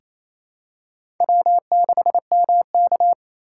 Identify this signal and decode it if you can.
Morse code, heard as a keyed tone, about 28 wpm, 700 Hz. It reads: W6MK
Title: USA Call Signs - 4 Characters - 2x Character Spacing - 28wpm